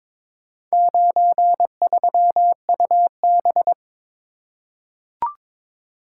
Morse code, heard as a keyed tone, about 22 wpm, 700 Hz. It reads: 93UB
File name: practice / 22wpm / Four Letter-Number Combinations - 22wpm